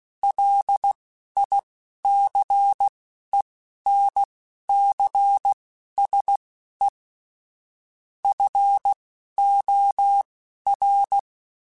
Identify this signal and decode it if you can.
Morse code, heard as a keyed tone, about 16 wpm, spaced as for 11 wpm, 800 Hz. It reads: LICENCSE FOR
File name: QSOs / QSOs-mp3s / kc7sac@10